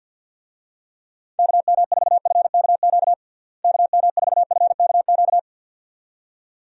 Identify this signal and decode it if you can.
Morse code, heard as a keyed tone, about 52 wpm, 700 Hz. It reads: KM4FKQ KM4FKQ
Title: USA Call Signs - 6 Characters - Repeated 2x - 50wpm